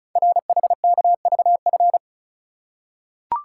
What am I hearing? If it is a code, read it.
Morse code, heard as a keyed tone, about 36 wpm, 700 Hz. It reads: RHKVF
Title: Code Groups - 5 Letters - 35wpm